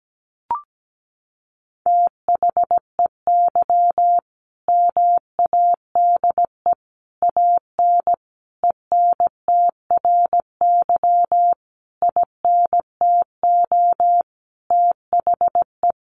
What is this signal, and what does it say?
Morse code, heard as a keyed tone, about 17 wpm, 700 Hz. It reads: THEY MADE AN ENTRY INTO THE
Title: Sentences from Top 1500 Words 17wpm